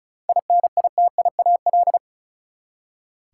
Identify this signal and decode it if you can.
Morse code, heard as a keyed tone, about 36 wpm, 700 Hz. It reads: INITIAL
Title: Seven Letter Words - 35wpm